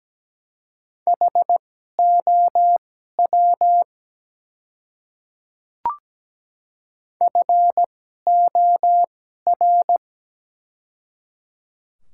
Morse code, heard as a keyed tone, about 17 wpm, spaced as for 12 wpm, 700 Hz. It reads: HOW FOR
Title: 2x Character Spacing - Top 100 Words - 17wpm